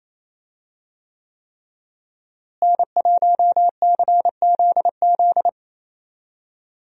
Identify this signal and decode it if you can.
Morse code, heard as a keyed tone, about 27 wpm, 700 Hz. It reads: N1CZZ
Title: USA Call Signs - 5 Characters - Repeated 3x - 28wpm